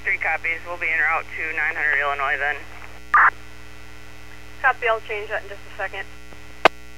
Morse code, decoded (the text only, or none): none